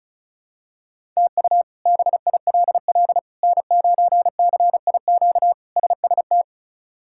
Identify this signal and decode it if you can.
Morse code, heard as a keyed tone, about 34 wpm, 700 Hz. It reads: TU BILL N9CIQ SST
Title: K1USN SST QSO Practice - 35wpm